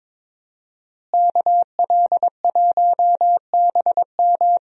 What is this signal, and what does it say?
Morse code, heard as a keyed tone, about 22 wpm, 700 Hz. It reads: KL1BM
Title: USA Call Signs - 5 Characters - Repeated 3x - 22wpm